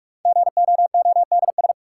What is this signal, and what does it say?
Morse code, heard as a keyed tone, about 42 wpm, 700 Hz. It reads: GOODS
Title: Sentences from Top 1000 Words - Review Part 6 - New Word Drill - 45wpm